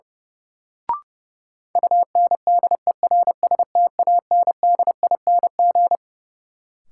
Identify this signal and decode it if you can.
Morse code, heard as a keyed tone, about 29 wpm, 700 Hz. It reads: UNDERSTANDING
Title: Sentences from Top 1000 Words - Review Part 3 - New Word Drill - 30wpm